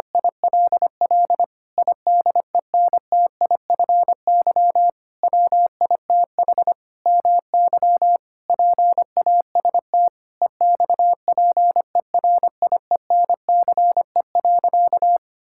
Morse code, heard as a keyed tone, about 25 wpm, 700 Hz. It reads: ILL IDENTIFY WITH MY PAST EXPERIENCE.
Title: Sentences from Top 1000 Words - Review Part 6 - 25wpm